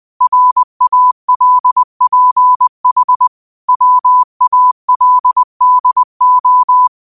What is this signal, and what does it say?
Morse code, heard as a keyed tone, about 20 wpm, 1000 Hz. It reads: RALPH WALDO
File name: qotdfiles / 20wpm